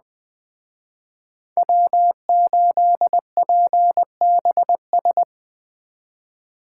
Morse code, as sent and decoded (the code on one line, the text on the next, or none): .-- ---.. .--. -... ...
W8PBS